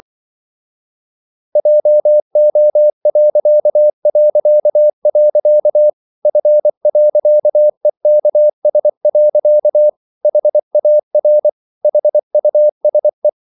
Morse code, heard as a keyed tone, about 24 wpm, 600 Hz. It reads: JO... F.EKS. HAR HUSE
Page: Ferie 24 wpm